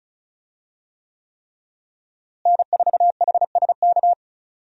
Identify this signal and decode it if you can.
Morse code, heard as a keyed tone, about 36 wpm, 700 Hz. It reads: N4HSK
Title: USA Call Signs - 5 Characters - Repeated 3x - 35wpm